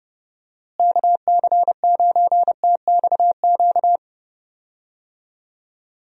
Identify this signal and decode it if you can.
Morse code, heard as a keyed tone, about 32 wpm, 700 Hz. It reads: KC9TXQ
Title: USA Call Signs - 6 Characters - Repeated 2x - 30wpm